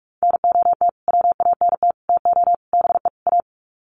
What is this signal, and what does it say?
Morse code, heard as a keyed tone, about 44 wpm, 700 Hz. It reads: NOT WANT TO BE A
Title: Sets of 5 Words Encore 45wpm